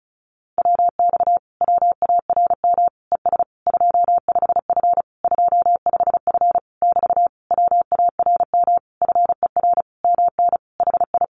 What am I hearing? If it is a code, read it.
Morse code, heard as a keyed tone, about 34 wpm, 700 Hz. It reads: WX WARM ES 25F 25F <BT> WARM FER MN HI
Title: Common QSO Phrases Encore 35wpm